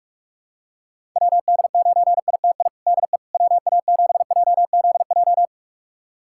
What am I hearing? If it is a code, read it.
Morse code, heard as a keyed tone, about 46 wpm, 700 Hz. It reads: WD9ITI DE WA7JZJ
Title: USA Callsign DE Callsign - 6 Characters - 45wpm